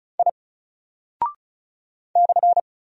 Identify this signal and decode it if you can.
Morse code, heard as a keyed tone, about 36 wpm, 700 Hz. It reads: I /
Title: Single Character (Letter:Number:Punctuation:Prosign) Alt 35wpm